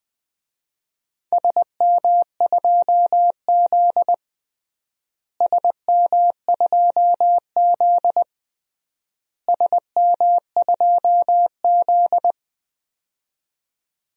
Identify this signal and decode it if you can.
Morse code, heard as a keyed tone, about 20 wpm, 700 Hz. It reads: SM2Z SM2Z SM2Z
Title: DXCC-100 Call Signs - 4 Characters - Repeated 3x - 20wpm